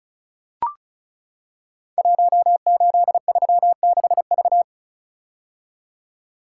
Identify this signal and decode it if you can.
Morse code, heard as a keyed tone, about 36 wpm, 690 Hz. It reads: 1836V
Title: Code Groups - 5 Letter:Numbers - 35wpm